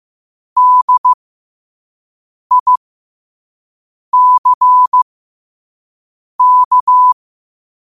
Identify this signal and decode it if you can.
Morse code, heard as a keyed tone, about 15 wpm, spaced as for 5 wpm, 1000 Hz. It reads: DICK